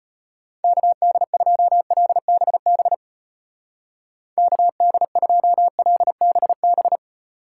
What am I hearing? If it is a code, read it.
Morse code, heard as a keyed tone, about 29 wpm, 700 Hz. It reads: KD2LBB KD2LBB
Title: Speed Racing - USA 2x3 Call Signs 25wpm